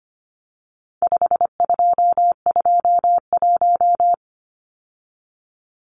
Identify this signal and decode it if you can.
Morse code, heard as a keyed tone, about 25 wpm, 700 Hz. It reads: 5221
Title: Numbers - 4-Digit 25wpm